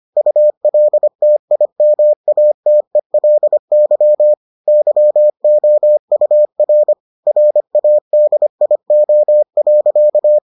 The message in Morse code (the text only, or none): ULTIMATELY YOUR RADIO.